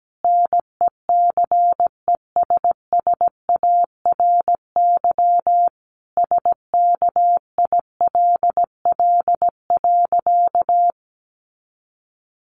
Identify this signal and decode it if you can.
Morse code, heard as a keyed tone, about 17 wpm, 700 Hz. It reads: NECESSARY SKILL.
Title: Sentences from Top 1200 Words 17wpm